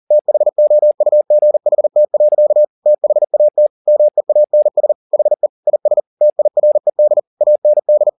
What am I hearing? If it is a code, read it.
Morse code, heard as a keyed tone, about 40 wpm, 600 Hz. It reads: THOUGHT. THAT MEANS HE IS TIRED AND
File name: the old man and the sea 2 40wpm